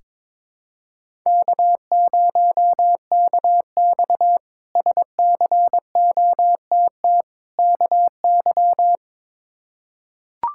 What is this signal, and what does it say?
Morse code, heard as a keyed tone, about 22 wpm, 700 Hz. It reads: K0KX SCOTT KY
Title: K1USN SST QSO Practice - 22wpm